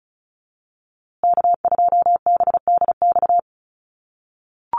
Morse code, heard as a keyed tone, about 34 wpm, 700 Hz. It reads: K2BDX
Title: USA Call Signs Repeated 2x 35wpm